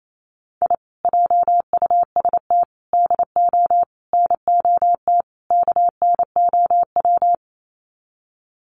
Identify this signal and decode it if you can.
Morse code, heard as a keyed tone, about 28 wpm, 700 Hz. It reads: I JUST DO NOT KNOW
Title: Sets of 5 Words Encore 28wpm